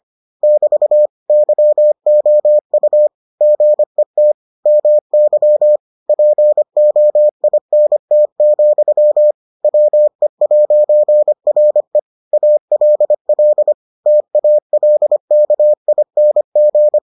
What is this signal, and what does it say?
Morse code, heard as a keyed tone, about 25 wpm, 600 Hz. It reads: <BT> YOU GET MY POINT, WE'RE ALL TALKING